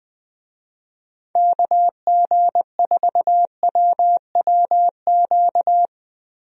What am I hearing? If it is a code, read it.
Morse code, heard as a keyed tone, about 20 wpm, 700 Hz. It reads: KG4WWQ
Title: USA Call Signs - 6 Characters - 20wpm